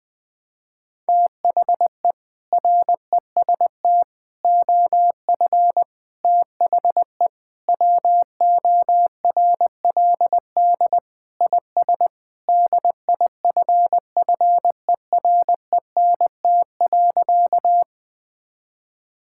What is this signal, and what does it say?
Morse code, heard as a keyed tone, about 20 wpm, 700 Hz. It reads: THE REST OF THE WORLD IS DIFFERENT.
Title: Sentences with 4-Word Phrases - 20wpm